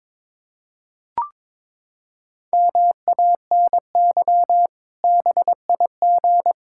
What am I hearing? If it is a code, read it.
Morse code, heard as a keyed tone, about 22 wpm, 700 Hz. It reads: MANY BIG
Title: Pangram Challenge v2 - 22wpm